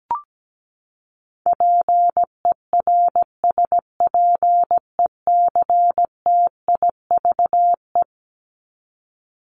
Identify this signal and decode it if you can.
Morse code, heard as a keyed tone, about 17 wpm, 700 Hz. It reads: PERSPECTIVE
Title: Top 1001-1250 Words 17wpm